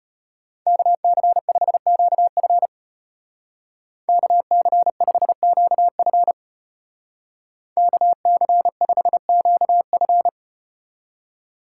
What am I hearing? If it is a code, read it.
Morse code, heard as a keyed tone, about 35 wpm, 700 Hz. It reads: KC5QF KC5QF KC5QF